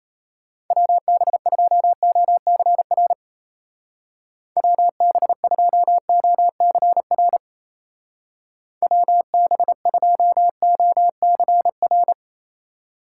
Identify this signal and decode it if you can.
Morse code, heard as a keyed tone, about 34 wpm, 700 Hz. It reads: WB2OCR WB2OCR WB2OCR
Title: Speed Racing - USA 2x3 Call Signs 28wpm